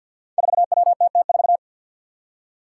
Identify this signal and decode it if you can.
Morse code, heard as a keyed tone, about 46 wpm, 700 Hz. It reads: 4WTT4
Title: Five Challenging Letter-Number Combinations 50wpm